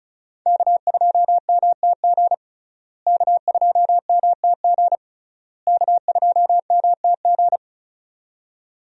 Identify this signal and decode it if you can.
Morse code, heard as a keyed tone, about 34 wpm, 700 Hz. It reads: K2MTG K2MTG K2MTG
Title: USA Call Signs - 5 Characters - Repeated 3x - 35wpm